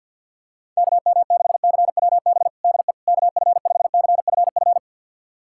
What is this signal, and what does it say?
Morse code, heard as a keyed tone, about 52 wpm, 700 Hz. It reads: KM6CWB DE KF5CFF
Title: USA Callsign DE Callsign 50wpm